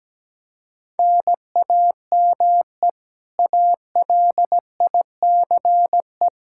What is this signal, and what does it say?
Morse code, heard as a keyed tone, about 17 wpm, 700 Hz. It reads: NAME ALICE
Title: Practice QSO - Exchange 1 - Beginning - 17wpm